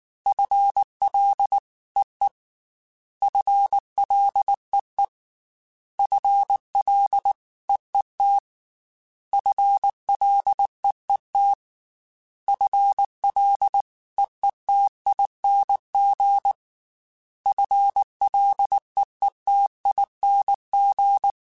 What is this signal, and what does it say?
Morse code, heard as a keyed tone, about 19 wpm, 800 Hz. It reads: FLEE FLEE FLEET FLEET FLEETING FLEETING
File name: FL / FL mp3s / FL Prefixes 3x2 sp @18